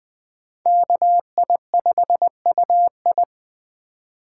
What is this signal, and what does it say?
Morse code, heard as a keyed tone, about 20 wpm, 700 Hz. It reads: KI5UI
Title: Speed Racing - USA 2x2 Call Signs 17wpm